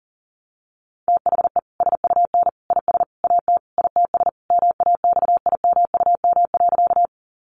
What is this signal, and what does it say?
Morse code, heard as a keyed tone, about 40 wpm, 700 Hz. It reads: THE SUN IS AT ITS MAXIMUM.
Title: Sentences from Top 1300 Words 40wpm